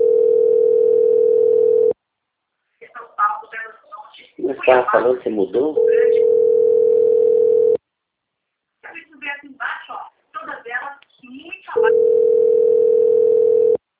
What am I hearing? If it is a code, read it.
Ringback tone (North America), 440 + 480 Hz, with speech.
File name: rg-1600-unknown- - - .2506